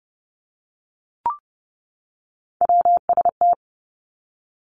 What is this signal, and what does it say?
Morse code, heard as a keyed tone, about 30 wpm, 700 Hz. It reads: WST
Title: Three Challenging Letter-Number Combinations 30wpm